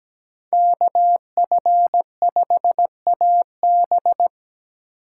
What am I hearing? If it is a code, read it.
Morse code, heard as a keyed tone, about 17 wpm, 700 Hz. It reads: KF5AB
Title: USA Call Signs - 2x Character Spacing - 17wpm